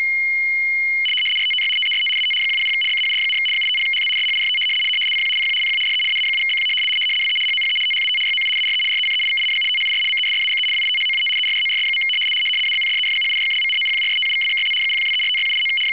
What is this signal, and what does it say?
RTTY, about 100 baud. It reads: RTTY TEST AT 100 BAUD, 850HZ SHIFT CQ CQ CQ DE AA7BQ WE HOPE THAT YOU ARE RECEIVING THIS MESSAGE WITHOUT ERRORS. ABCDEFGHIJKLMNOPQRSTUVWXYZ 0123456789
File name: w ry850 / rtty100